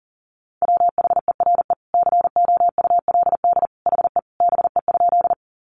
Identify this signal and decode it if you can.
Morse code, heard as a keyed tone, about 40 wpm, 700 Hz. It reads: WHERE COULD HE BE?
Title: Sentences from Top 200 Words 40wpm